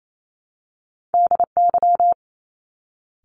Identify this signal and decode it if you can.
Morse code, heard as a keyed tone, about 28 wpm, 700 Hz. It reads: DY
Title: Top Two Letter Combinations in Words 28wpm